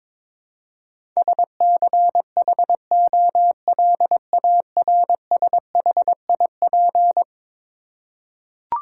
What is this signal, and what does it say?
Morse code, heard as a keyed tone, about 22 wpm, 700 Hz. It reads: SCHOLARSHIP
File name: practice / 22wpm / Eleven Letter Words - 22wpm